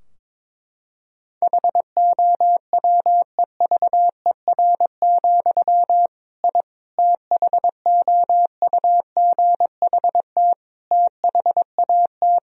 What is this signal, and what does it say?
Morse code, heard as a keyed tone, about 21 wpm, 700 Hz. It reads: HOWEVER, I THOUGHT THAT
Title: Sentences from Top 1000 Words - Review Part 3 - 22wpm